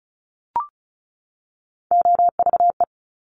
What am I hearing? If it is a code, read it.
Morse code, heard as a keyed tone, about 36 wpm, 700 Hz. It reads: OVE
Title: Top Three Letter Combinations in Words 35wpm